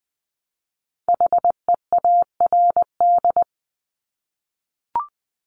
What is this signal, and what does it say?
Morse code, heard as a keyed tone, about 20 wpm, 700 Hz. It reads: HEARD
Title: Multi-letter Phonemes Repeated 3x 20wpm